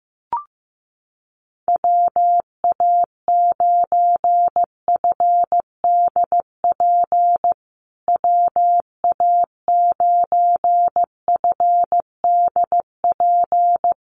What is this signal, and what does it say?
Morse code, heard as a keyed tone, about 15 wpm, 700 Hz. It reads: WA9FDP WA9FDP
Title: USA Call Signs Repeated 2x 15wpm